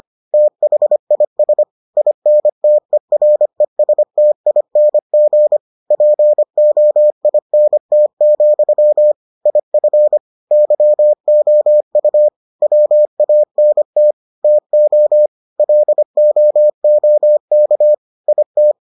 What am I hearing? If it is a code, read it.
Morse code, heard as a keyed tone, about 25 wpm, 600 Hz. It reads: THIS INTERESTING POINT, IF YOU WANT TO LOOK IT